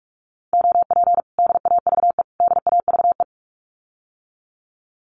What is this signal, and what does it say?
Morse code, heard as a keyed tone, about 46 wpm, 700 Hz. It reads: OP DAVE DAVE
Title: Common QSO Phrases Encore 45wpm